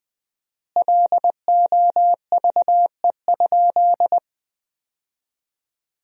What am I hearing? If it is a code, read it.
Morse code, heard as a keyed tone, about 21 wpm, 700 Hz. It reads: LOVE?
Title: 4x Word-Spacing - Sentences from Top 500 Words - 20wpm